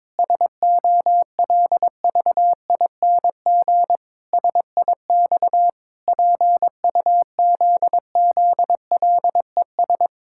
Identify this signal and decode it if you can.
Morse code, heard as a keyed tone, about 22 wpm, 700 Hz. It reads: SOLVING SIX PUZZLES
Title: Pangram Challenge v2 - 22wpm